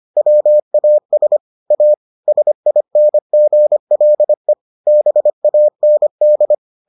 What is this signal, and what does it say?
Morse code, heard as a keyed tone, about 25 wpm, 600 Hz. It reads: WAS A SINGLE BAND